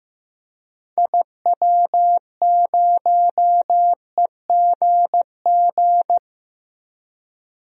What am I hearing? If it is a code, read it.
Morse code, heard as a keyed tone, about 15 wpm, 700 Hz. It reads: IW0EGG
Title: DXCC-100 Call Signs - 6 Characters - Repeated 2x - 15wpm